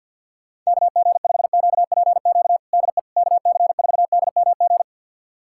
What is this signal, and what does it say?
Morse code, heard as a keyed tone, about 52 wpm, 700 Hz. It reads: KG5QPQ DE KC4DMG
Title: USA Callsign DE Callsign - 6 Characters - 50wpm